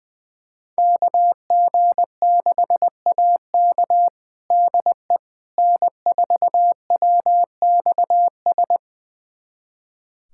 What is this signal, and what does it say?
Morse code, heard as a keyed tone, about 20 wpm, 700 Hz. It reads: KG6AK DE N4WXS
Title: USA Callsign DE Callsign - 5 Characters - 20wpm